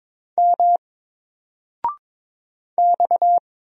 Morse code, heard as a keyed tone, about 22 wpm, 700 Hz. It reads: M X
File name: Single-Letter-Number-22wpm